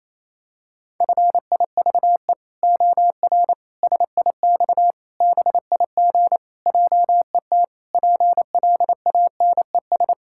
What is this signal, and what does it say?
Morse code, heard as a keyed tone, about 28 wpm, 700 Hz. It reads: FIVE OR SIX BIG JET PLANES
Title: Pangram Challenge v2 - 28wpm